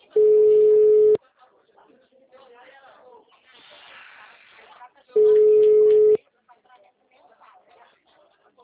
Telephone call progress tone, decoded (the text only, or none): ringback tone